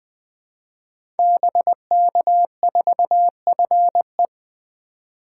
Morse code, heard as a keyed tone, about 20 wpm, 700 Hz. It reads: BK4FE
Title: Code Groups - 5 Letter:Numbers - 20wpm